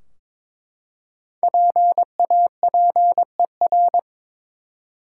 Morse code, heard as a keyed tone, about 23 wpm, 700 Hz. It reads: PAPER
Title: Five Letter Words - 22wpm